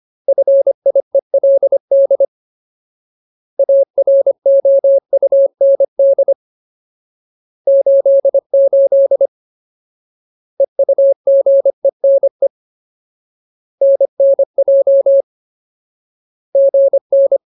Morse code, heard as a keyed tone, about 25 wpm, 550 Hz. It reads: FIELD AROUND 88 EUGENE NNJ GN
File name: Ultimate Word List / 25WPM Ultimate